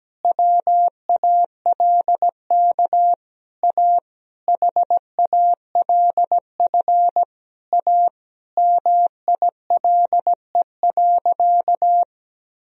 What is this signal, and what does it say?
Morse code, heard as a keyed tone, about 17 wpm, 700 Hz. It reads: WALK A HALF A MILE.
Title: Sentences from Top 1000 Words - Review Part 1 - 17wpm